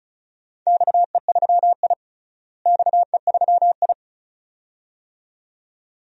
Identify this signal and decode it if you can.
Morse code, heard as a keyed tone, about 36 wpm, 700 Hz. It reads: XE3I XE3I
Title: DXCC-100 Call Signs - 4 Characters - Repeated 2x - 35wpm